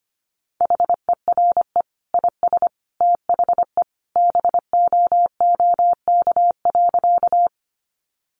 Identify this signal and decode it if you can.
Morse code, heard as a keyed tone, about 25 wpm, 700 Hz. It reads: HERE IS THE BOOK.
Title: Sentences from Top 300 Words 25wpm